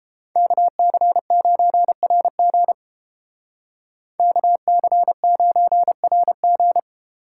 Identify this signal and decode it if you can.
Morse code, heard as a keyed tone, about 32 wpm, 700 Hz. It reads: KC9RG KC9RG